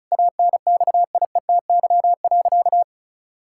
Morse code, heard as a keyed tone, about 36 wpm, 690 Hz. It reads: ANXIETY.
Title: Sentences from Top 1600 Words 35wpm